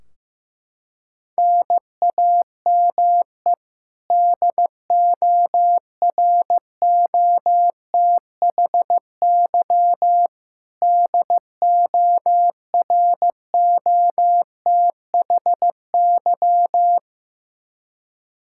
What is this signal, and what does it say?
Morse code, heard as a keyed tone, about 15 wpm, 700 Hz. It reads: NAME DOROTHY DOROTHY
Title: Practice QSO - Exchange 1 - Beginning - 15wpm